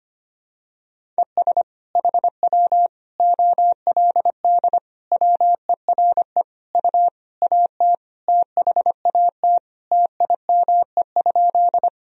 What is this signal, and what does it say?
Morse code, heard as a keyed tone, about 25 wpm, 700 Hz. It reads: ES HW OLD WERE U AT THAT TIME?
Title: Sentences from Top 200 Words and Abbreviations 25wpm